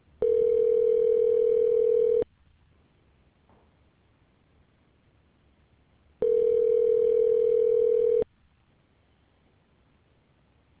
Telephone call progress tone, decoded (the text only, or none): ringback tone